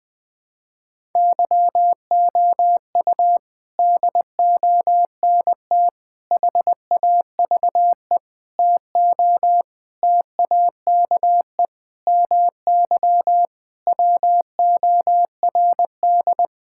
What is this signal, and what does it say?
Morse code, heard as a keyed tone, about 21 wpm, 700 Hz. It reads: YOU DONT HAVE TO TAKE MY WORD
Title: Sentences with 8-Word Phrases - 20wpm